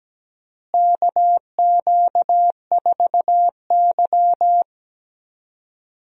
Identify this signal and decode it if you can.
Morse code, heard as a keyed tone, about 17 wpm, 700 Hz. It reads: KQ4Y
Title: USA Call Signs - 4 Characters - Repeated 2x - 17wpm